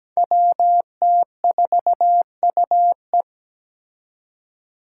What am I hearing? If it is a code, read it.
Morse code, heard as a keyed tone, about 17 wpm, 700 Hz. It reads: WT4UE
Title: Five Challenging Letter-Number Combinations 17wpm